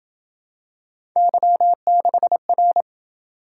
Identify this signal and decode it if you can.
Morse code, heard as a keyed tone, about 28 wpm, 700 Hz. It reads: Y6R